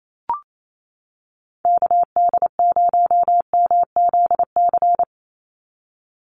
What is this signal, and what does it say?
Morse code, heard as a keyed tone, about 28 wpm, 700 Hz. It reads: KD0MZC
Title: USA Call Signs 28wpm